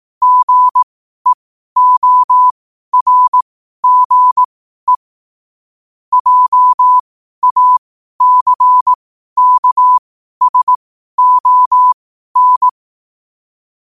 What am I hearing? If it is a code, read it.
Morse code, heard as a keyed tone, about 18 wpm, spaced as for 12 wpm, 1000 Hz. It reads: GEORGE JACKSON
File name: qotdfiles / 12wpm